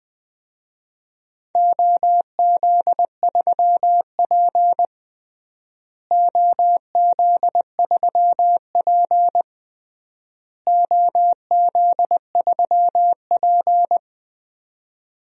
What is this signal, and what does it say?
Morse code, heard as a keyed tone, about 21 wpm, 700 Hz. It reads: OZ3P OZ3P OZ3P
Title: DXCC-100 Call Signs - 4 Characters - Repeated 3x - 20wpm